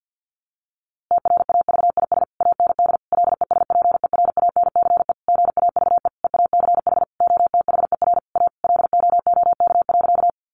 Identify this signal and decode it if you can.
Morse code, heard as a keyed tone, about 50 wpm, 700 Hz. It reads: TRAVIS AND LESPERANCE GAVE EACH OTHER A LOOK.
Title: A Sound Of Thunder 50wpm